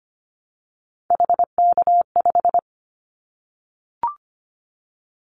Morse code, heard as a keyed tone, about 25 wpm, 700 Hz. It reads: HK5
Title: Callsign Prefixes for Top 100 DXCC - 25wpm